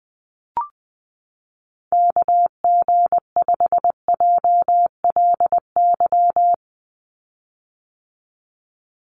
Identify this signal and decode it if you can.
Morse code, heard as a keyed tone, about 19 wpm, 700 Hz. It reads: KG5JLY
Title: USA Call Signs 20wpm